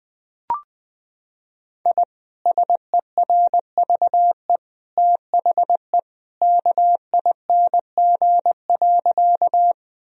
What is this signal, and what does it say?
Morse code, heard as a keyed tone, about 20 wpm, 700 Hz. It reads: I SERVE THE KING.
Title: Sentences from Top 1000 Words - Review Part 2 - 20wpm